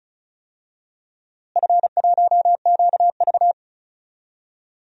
Morse code, heard as a keyed tone, about 36 wpm, 700 Hz. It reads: F1QV
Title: DXCC-100 Call Signs - 4 Characters - Repeated 2x - 35wpm